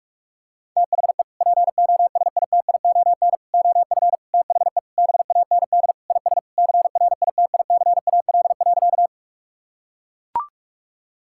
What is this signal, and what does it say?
Morse code, heard as a keyed tone, about 41 wpm, 700 Hz. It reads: THE POSITION OF THE BAND IS CRITICAL.
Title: Sentences from Top 1600 Words 45wpm